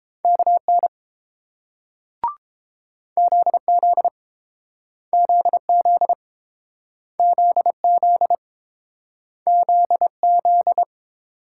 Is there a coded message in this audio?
Yes: Morse code from 0.2 to 10.8 s, heard as a keyed tone, about 30 wpm, 700 Hz. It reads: KN ZZ ZZ ZZ ZZ